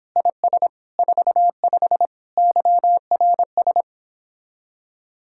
Morse code, heard as a keyed tone, about 29 wpm, 700 Hz. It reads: IS 45 YRS